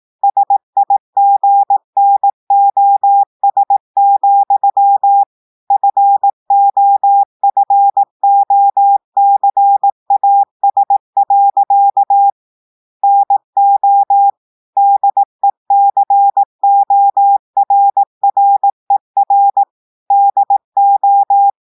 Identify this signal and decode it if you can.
Morse code, heard as a keyed tone, about 18 wpm, 800 Hz. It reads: SIGNOS, FOFOCAS. NO DECORRER DO
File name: audios / portugais / portugais texte en clair 2 vitesse 18 mots minutes